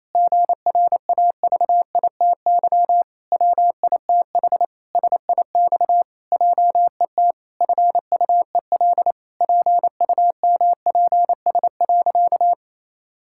Morse code, heard as a keyed tone, about 28 wpm, 700 Hz. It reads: GRAVITY WITH SIX JET FUEL PUMPS.
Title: Pangram Challenge v2 - 28wpm